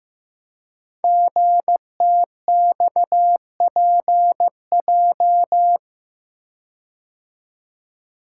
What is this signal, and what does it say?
Morse code, heard as a keyed tone, about 15 wpm, 700 Hz. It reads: GTXPJ
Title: Code Groups - 5 Letters - 15wpm